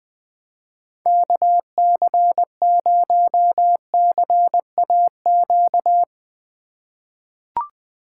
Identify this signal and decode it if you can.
Morse code, heard as a keyed tone, about 20 wpm, 700 Hz. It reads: KC0CAQ
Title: USA Call Signs - 6 Characters - 8x Character Spacing - 20wpm